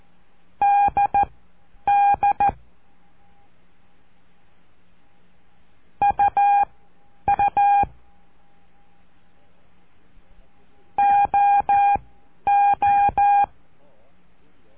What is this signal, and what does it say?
Morse code, heard as a keyed tone, about 14 wpm, spaced as for 9 wpm, 810 Hz. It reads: DD UU OO
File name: cursos / CURSO DE TELEGRAFIA / Telegrafia - Trilha 05